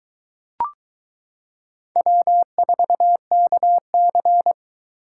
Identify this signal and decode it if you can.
Morse code, heard as a keyed tone, about 22 wpm, 700 Hz. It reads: W4KC